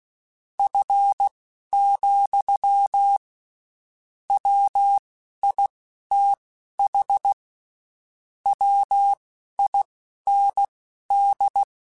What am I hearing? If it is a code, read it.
Morse code, heard as a keyed tone, about 16 wpm, spaced as for 11 wpm, 800 Hz. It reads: F, WITH WIND
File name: Spaced Characters / Spaced Characters-mp3s / Weather4 21 sp @15